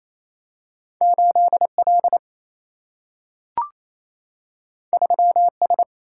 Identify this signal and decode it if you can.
Morse code, heard as a keyed tone, about 28 wpm, 700 Hz. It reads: 8L 3S
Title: Single-Letter Single-Number 28wpm